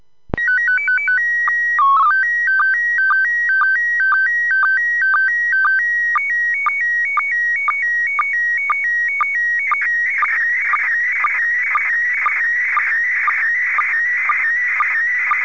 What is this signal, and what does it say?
SSTV (slow-scan television), PD 120.
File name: SSTV-Signal-31.02.19-17.35Uhr-JN39vk-via-DL0HSK-A